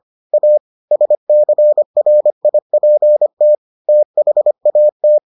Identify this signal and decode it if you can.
Morse code, heard as a keyed tone, about 25 wpm, 600 Hz. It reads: A SCRIPT THAT